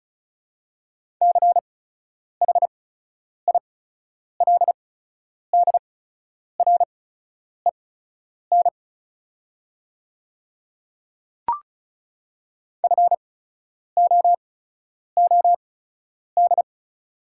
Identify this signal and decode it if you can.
Morse code, heard as a keyed tone, about 35 wpm, spaced as for 10 wpm, 700 Hz. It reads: CHILDREN FOOD
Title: 8x Character Spacing - Top 300 Words - 35wpm